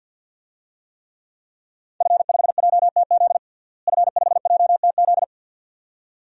Morse code, heard as a keyed tone, about 51 wpm, 700 Hz. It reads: F5JTZ F5JTZ
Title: DXCC-100 Call Signs - 5 Characters - Repeated 2x - 50wpm